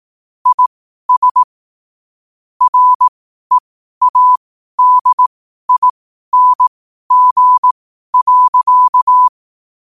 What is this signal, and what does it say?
Morse code, heard as a keyed tone, about 18 wpm, spaced as for 12 wpm, 1000 Hz. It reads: IS READING.
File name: qotdfiles / 12wpm